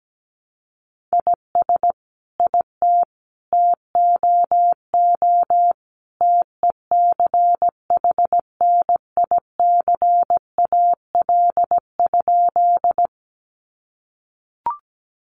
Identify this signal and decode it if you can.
Morse code, heard as a keyed tone, about 17 wpm, 700 Hz. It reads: IS IT TOO TECHNICAL?
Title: Sentences from Top 1100 Words 17wpm